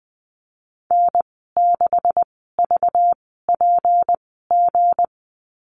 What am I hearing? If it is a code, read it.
Morse code, heard as a keyed tone, about 19 wpm, spaced as for 14 wpm, 700 Hz. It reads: N6VPG
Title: 231 Lesson 37g - Letter x - Call Signs 20wpm